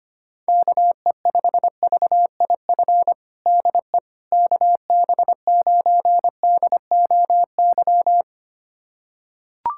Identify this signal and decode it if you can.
Morse code, heard as a keyed tone, about 25 wpm, 700 Hz. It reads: KE5VIF DE KB9DOY
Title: USA Callsign DE Callsign - 6 Characters - 25wpm